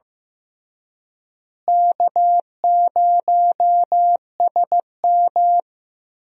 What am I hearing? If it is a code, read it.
Morse code, heard as a keyed tone, about 15 wpm, 700 Hz. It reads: K0SM
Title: USA Call Signs - 4 Characters - Repeated 2x - 15wpm